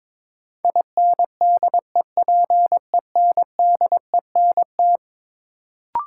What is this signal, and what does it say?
Morse code, heard as a keyed tone, about 22 wpm, 700 Hz. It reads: INDEPENDENT
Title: Sentences from Top 900 Words - New Word Drill - 22wpm